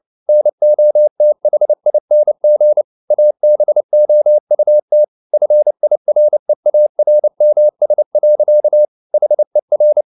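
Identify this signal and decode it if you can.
Morse code, heard as a keyed tone, about 29 wpm, 600 Hz. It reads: NOTHING ABOUT FIREARMS. HER
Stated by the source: Morse code at 29 wpm, sending Ken Follett's novel Never in English, from CW med Gnister.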